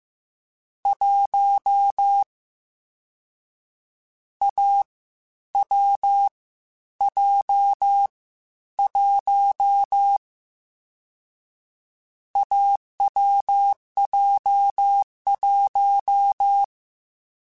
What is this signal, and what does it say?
Morse code, heard as a keyed tone, about 15 wpm, 800 Hz. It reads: 1 A W J 1 AWJ1
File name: Difficult Combos / Difficult Combos-mp3s / AWJ1 @13